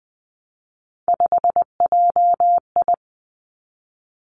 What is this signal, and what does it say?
Morse code, heard as a keyed tone, about 19 wpm, 700 Hz. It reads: 5JI
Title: Three Challenging Letter-Number Combinations 20wpm